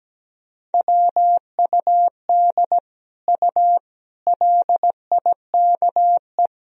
Morse code, heard as a keyed tone, about 17 wpm, 700 Hz. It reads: WUD U LIKE
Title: Sentences from Top 100 Words and Abbreviations 17wpm